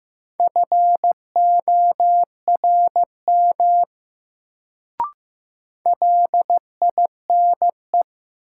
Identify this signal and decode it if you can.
Morse code, heard as a keyed tone, about 15 wpm, 700 Hz. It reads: FORM LINE
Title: Sentences from Top 1000 Words - Review Part 1 - New Word Drill - 15wpm